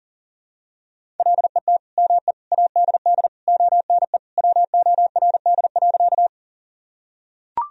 Morse code, heard as a keyed tone, about 40 wpm, 700 Hz. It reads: LET ME ADD ONE WORD.
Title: Sentences from Top 1000 Words - Review Part 1 - 40wpm